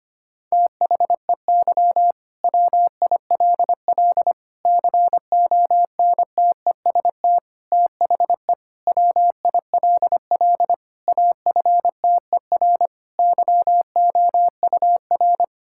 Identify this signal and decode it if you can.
Morse code, heard as a keyed tone, about 25 wpm, 700 Hz. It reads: THEY WILL CONTEST THE WILL AFTER YOUR
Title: Sentences from Top 1600 Words 25wpm